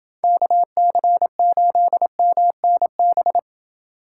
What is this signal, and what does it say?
Morse code, heard as a keyed tone, about 29 wpm, 700 Hz. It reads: KC8MNB